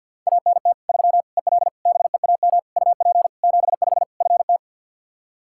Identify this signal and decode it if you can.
Morse code, heard as a keyed tone, about 52 wpm, 700 Hz. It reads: ANT 3 EL BEAM UP 75 FT
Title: Practice QSO - Exchange 2 - Info - 50wpm